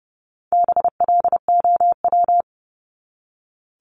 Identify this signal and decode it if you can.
Morse code, heard as a keyed tone, about 30 wpm, 700 Hz. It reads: BLOW
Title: Top 1751-2000 Words 30wpm